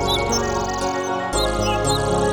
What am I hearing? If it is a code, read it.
There is no code.